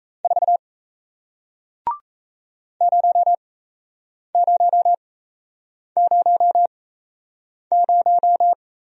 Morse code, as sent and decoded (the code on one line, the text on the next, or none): ....- / ----- ----- ----- -----
4 0000